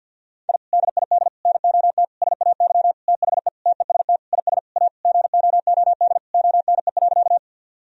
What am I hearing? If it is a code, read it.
Morse code, heard as a keyed tone, about 50 wpm, 700 Hz. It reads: I DID NOT SAY THE TEST IS A GOOD ONE.
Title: Sentences from Top 1000 Words - Review Part 3 - 50wpm